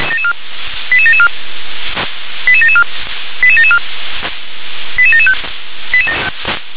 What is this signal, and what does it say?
Five-tone selective call (ZVEI), heard as quick runs of five tones, tones of about 70 ms. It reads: 88774 88774 88774 88775